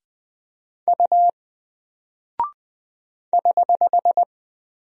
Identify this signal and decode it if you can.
Morse code, heard as a keyed tone, about 20 wpm, 700 Hz. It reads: U <HH>
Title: Single Character (Letter:Number:Punctuation:Prosign) Alt 20wpm